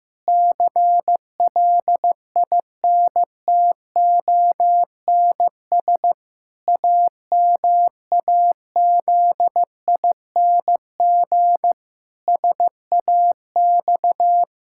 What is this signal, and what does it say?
Morse code, heard as a keyed tone, about 15 wpm, 700 Hz. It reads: CLINTONS AMAZING SAX
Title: Pangram Challenge v2 - 15wpm